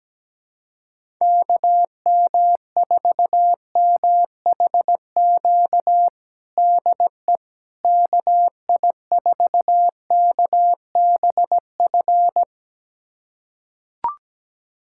Morse code, heard as a keyed tone, about 17 wpm, 700 Hz. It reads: KM4MHQ DE KI4KBF
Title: USA Callsign DE Callsign - 6 Characters - 17wpm